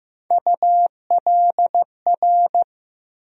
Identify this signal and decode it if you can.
Morse code, heard as a keyed tone, about 15 wpm, 700 Hz. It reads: ULR
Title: USA Call Sign Postfixes - 3 Letters 15wpm